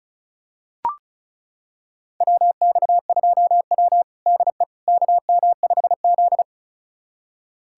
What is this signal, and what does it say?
Morse code, heard as a keyed tone, about 34 wpm, 700 Hz. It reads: WX2W DE KM5Z
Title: USA Callsign DE Callsign - 4 Characters - 35wpm